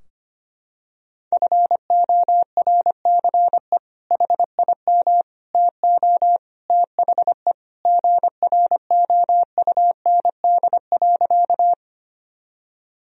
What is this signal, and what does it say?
Morse code, heard as a keyed tone, about 25 wpm, 700 Hz. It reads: FORCE HIM TO THE GROUND.
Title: Sentences from Top 1000 Words - Review Part 3 - 25wpm